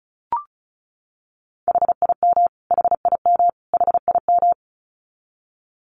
Morse code, heard as a keyed tone, about 34 wpm, 700 Hz. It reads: HIM HIM HIM
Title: Top 300 Words Repeated 35wpm